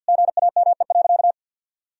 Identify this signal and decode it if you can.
Morse code, heard as a keyed tone, about 50 wpm, 700 Hz. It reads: GAME.
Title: Sentences from Top 1000 Words - Review Part 4 - 50wpm